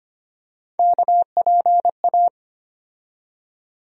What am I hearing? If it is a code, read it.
Morse code, heard as a keyed tone, about 25 wpm, 700 Hz. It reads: KPA
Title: USA Call Sign Postfixes - 3 Letters 25wpm